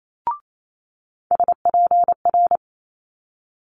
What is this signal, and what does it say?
Morse code, heard as a keyed tone, about 28 wpm, 700 Hz. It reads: SPR
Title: Multi-letter Phonemes Quick Fire 28wpm